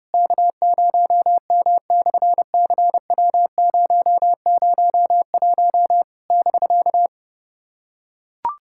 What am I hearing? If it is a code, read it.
Morse code, heard as a keyed tone, about 28 wpm, 700 Hz. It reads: K0M/CW001 <BK>
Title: SOTA QSO Practice 30wpm